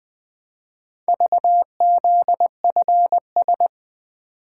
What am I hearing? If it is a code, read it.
Morse code, heard as a keyed tone, about 20 wpm, 700 Hz. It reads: VZFS
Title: Four Letter-Number Combinations - 20wpm